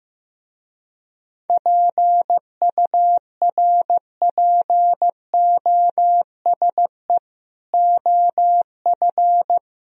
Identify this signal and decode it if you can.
Morse code, heard as a keyed tone, about 15 wpm, 700 Hz. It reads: PURPOSE OF
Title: Speed-Racing - Sets of 5 Words 15wpm